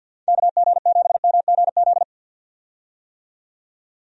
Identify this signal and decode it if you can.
Morse code, heard as a keyed tone, about 51 wpm, 700 Hz. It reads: KG7MGZ
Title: USA Call Signs - 6 Characters - 50wpm